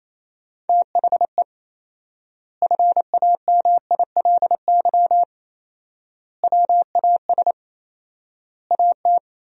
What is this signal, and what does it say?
Morse code, heard as a keyed tone, about 28 wpm, 700 Hz. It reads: THE FAMILY WAS AT
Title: 4x Word-Spacing - Sentences from Top 400 Words - 28wpm